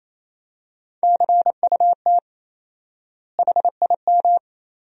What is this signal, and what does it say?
Morse code, heard as a keyed tone, about 28 wpm, 700 Hz. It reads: CUT HIM
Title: 4x Word-Spacing - Sentences from Top 300 Words - 28wpm